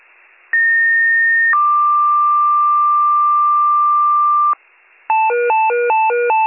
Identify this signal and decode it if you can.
Two-tone page, heard as a one-second tone, then a three-second tone, about 1850 then 1180 Hz. It is heard from 0.5 to 4.5 s.